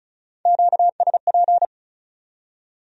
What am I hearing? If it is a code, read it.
Morse code, heard as a keyed tone, about 44 wpm, 700 Hz. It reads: QSP